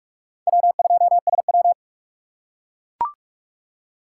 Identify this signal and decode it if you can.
Morse code, heard as a keyed tone, about 44 wpm, 700 Hz. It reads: W2SW
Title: USA Call Signs - 2x Character Spacing - 45wpm